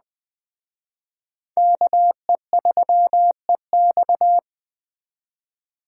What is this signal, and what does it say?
Morse code, heard as a keyed tone, about 20 wpm, 700 Hz. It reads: KE3EX